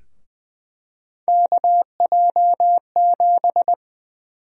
Morse code, heard as a keyed tone, about 21 wpm, 700 Hz. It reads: KJ7
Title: USA Call Sign Prefixes - 20wpm